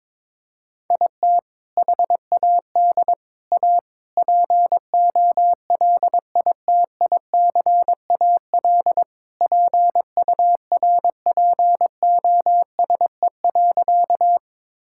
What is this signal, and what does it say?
Morse code, heard as a keyed tone, about 22 wpm, 700 Hz. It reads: IT HAD A POLITICAL PURPOSE.
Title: Sentences from Top 1000 Words - Review Part 5 - 22wpm